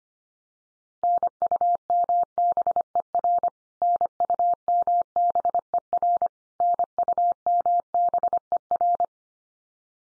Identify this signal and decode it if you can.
Morse code, heard as a keyed tone, about 25 wpm, 700 Hz. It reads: NUMBER NUMBER NUMBER
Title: Top 100 Words 25wpm x3